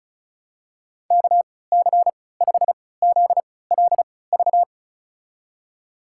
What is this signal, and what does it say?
Morse code, heard as a keyed tone, about 36 wpm, spaced as for 20 wpm, 700 Hz. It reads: KC5ZLV
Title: USA Call Signs - 3x Character Spacing - 35wpm